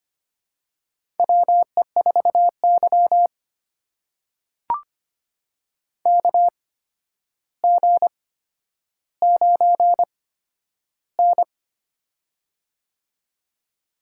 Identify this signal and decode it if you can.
Morse code, heard as a keyed tone, about 25 wpm, spaced as for 11 wpm, 700 Hz. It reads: WE4Y KG9N
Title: USA Call Signs - 4 Characters - 8x Character Spacing - 25wpm